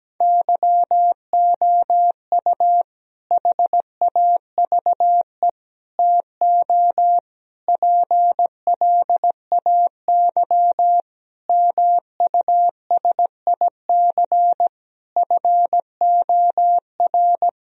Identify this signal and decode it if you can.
Morse code, heard as a keyed tone, about 17 wpm, 700 Hz. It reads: YOU HAVE TO PLAY MUSIC FOR
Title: Sentences from Top 1000 Words - Review Part 1 - 17wpm